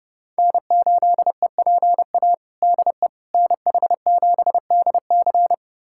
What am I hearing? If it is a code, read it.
Morse code, heard as a keyed tone, about 30 wpm, 700 Hz. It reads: N8EPA DE NH7DC
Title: USA Callsign DE Callsign - 5 Characters - 30wpm